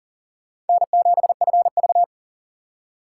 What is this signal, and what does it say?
Morse code, heard as a keyed tone, about 53 wpm, spaced as for 47 wpm, 700 Hz. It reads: N7FV
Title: Speed Racing - USA 1x2 Call Signs 40wpm